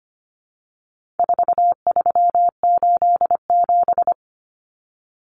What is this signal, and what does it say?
Morse code, heard as a keyed tone, about 25 wpm, 700 Hz. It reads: 4387
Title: Numbers - 4-Digit 25wpm